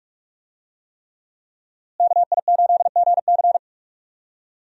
Morse code, heard as a keyed tone, about 46 wpm, 700 Hz. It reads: KI8GC
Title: USA Call Signs - 5 Characters - Repeated 3x - 45wpm